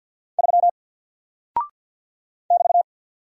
Morse code, heard as a keyed tone, about 48 wpm, 700 Hz. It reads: 3 <BT>
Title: Single Character (Letter:Number:Punctuation:Prosign) Alt 50wpm